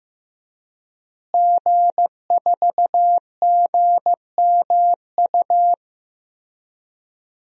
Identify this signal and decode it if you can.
Morse code, heard as a keyed tone, about 15 wpm, 700 Hz. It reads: G4GMU
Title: DXCC-100 Call Signs - 5 Characters - Repeated 3x - 15wpm